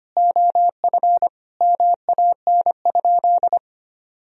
Morse code, heard as a keyed tone, about 25 wpm, 700 Hz. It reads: OF MAN?
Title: Sentences from Top 1000 Words - Review Part 4 - 25wpm